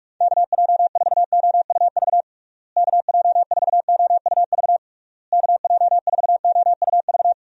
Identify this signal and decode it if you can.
Morse code, heard as a keyed tone, about 46 wpm, 700 Hz. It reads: KJ4OUV KJ4OUV KJ4OUV
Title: USA Call Signs - 6 Characters - Repeated 3x - 45wpm